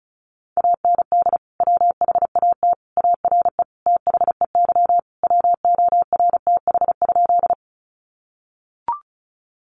Morse code, heard as a keyed tone, about 34 wpm, 700 Hz. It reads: AND WHAT ARE THEY WORTH?
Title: Sentences from Top 800 Words 35wpm